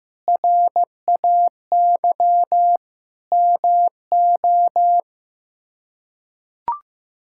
Morse code, heard as a keyed tone, about 15 wpm, 700 Hz. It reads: RAY MO
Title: K1USN SST QSO Practice - 15wpm